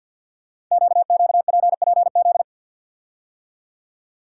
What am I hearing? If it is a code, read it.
Morse code, heard as a keyed tone, about 50 wpm, 700 Hz. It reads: QQPPZ
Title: Code Groups - 5 Letters - 50wpm